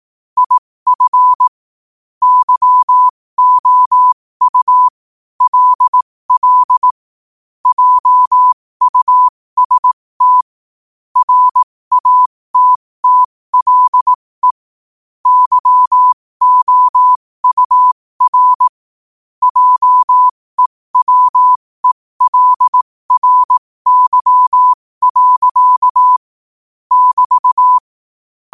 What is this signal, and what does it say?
Morse code, heard as a keyed tone, about 18 wpm, spaced as for 15 wpm, 1000 Hz. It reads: IF YOU LL JUST RATTLE YOUR JEWELRY. <BT>
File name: qotdfiles / 15wpm